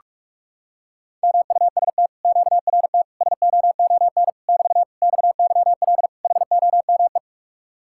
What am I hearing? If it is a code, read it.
Morse code, heard as a keyed tone, about 40 wpm, 700 Hz. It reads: MUST QRT SOON <BT> XYL HOME